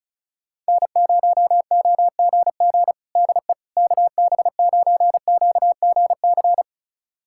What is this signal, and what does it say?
Morse code, heard as a keyed tone, about 36 wpm, 700 Hz. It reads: N0OGG DE KB9QGC
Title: USA Callsign DE Callsign 35wpm